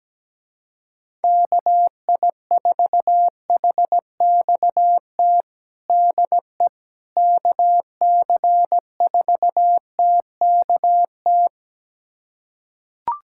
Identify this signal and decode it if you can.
Morse code, heard as a keyed tone, about 17 wpm, 700 Hz. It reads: KI4HXT DE KC4TKT
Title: USA Callsign DE Callsign - 6 Characters - 17wpm